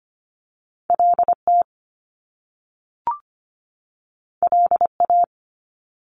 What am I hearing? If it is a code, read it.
Morse code, heard as a keyed tone, about 25 wpm, 700 Hz. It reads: LT LA
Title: Top Two Letter Combinations in Words 25wpm